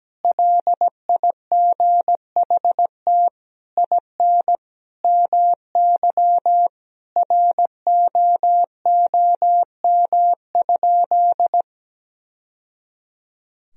Morse code, heard as a keyed tone, about 17 wpm, 700 Hz. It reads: LIGHT IN MY ROOM?
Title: Sentences from Top 1000 Words - Review Part 1 - 17wpm